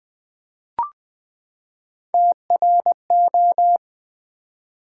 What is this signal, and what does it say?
Morse code, heard as a keyed tone, about 20 wpm, 700 Hz. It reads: TRO
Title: Build A Word - 3 Letters At A Time - 6-Letter Words - 20wpm